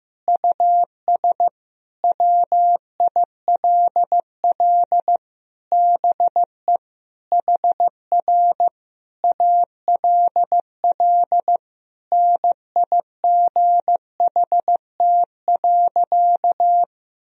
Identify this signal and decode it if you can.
Morse code, heard as a keyed tone, about 14 wpm, 700 Hz. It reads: US WILL BE HR ALL NIGHT.
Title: Sentences from Top 300 Words and Abbreviations 15wpm